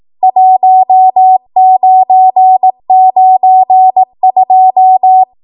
Morse code, heard as a keyed tone, about 18 wpm, 750 Hz. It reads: 1992